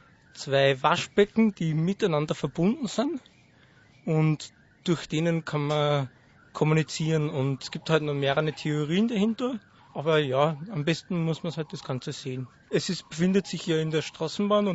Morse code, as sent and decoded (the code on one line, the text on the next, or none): none